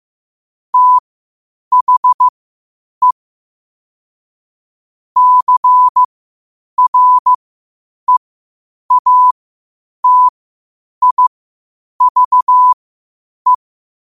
Morse code, heard as a keyed tone, about 15 wpm, spaced as for 8 wpm, 1000 Hz. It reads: THE CREATIVE
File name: qotdfiles / 8wpm